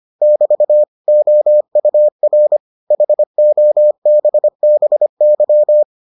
Morse code, heard as a keyed tone, about 25 wpm, 600 Hz. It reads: <BT> OUR HOBBY